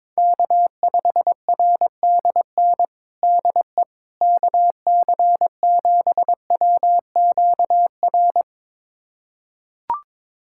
Morse code, heard as a keyed tone, about 22 wpm, 700 Hz. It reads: K5RDN DE KC7WQR
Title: USA Callsign DE Callsign 22wpm